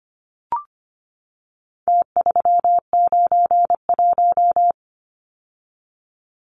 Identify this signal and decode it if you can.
Morse code, heard as a keyed tone, about 25 wpm, 700 Hz. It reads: T391
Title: Numbers - Contest Style - 4-Digit 25wpm